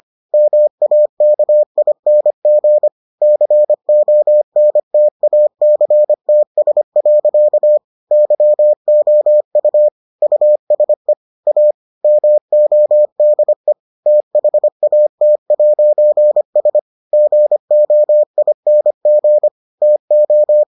Morse code, heard as a keyed tone, about 25 wpm, 600 Hz. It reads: MAKING CONTACTS. YOU USE A MODE THAT'S GOING TO